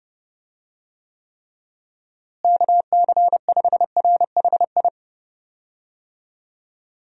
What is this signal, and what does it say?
Morse code, heard as a keyed tone, about 32 wpm, 700 Hz. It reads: KC5RHI
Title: USA Call Signs - 6 Characters - Repeated 3x - 30wpm